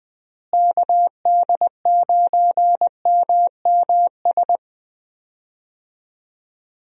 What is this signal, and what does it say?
Morse code, heard as a keyed tone, about 20 wpm, 700 Hz. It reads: KD9MMS
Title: USA Call Signs - 6 Characters - Repeated 3x - 20wpm